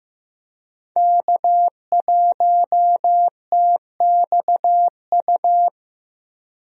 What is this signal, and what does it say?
Morse code, heard as a keyed tone, about 15 wpm, 700 Hz. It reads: K1TXU
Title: USA Call Signs - 5 Characters - Repeated 2x - 15wpm